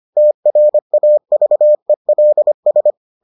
Morse code, heard as a keyed tone, about 25 wpm, 600 Hz. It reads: TRAVELS